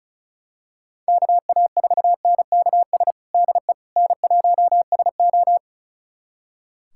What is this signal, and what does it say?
Morse code, heard as a keyed tone, about 34 wpm, 700 Hz. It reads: KA4NKS DE N1SO
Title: USA Callsign DE Callsign 35wpm